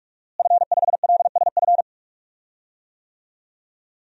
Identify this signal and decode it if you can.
Morse code, heard as a keyed tone, about 46 wpm, 700 Hz. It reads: F5LSF
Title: DXCC-100 Call Signs - 5 Characters - Repeated 3x - 45wpm